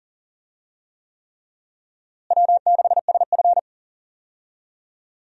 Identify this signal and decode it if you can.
Morse code, heard as a keyed tone, about 37 wpm, 700 Hz. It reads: W6SF
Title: USA Call Signs - 4 Characters - Repeated 3x - 40wpm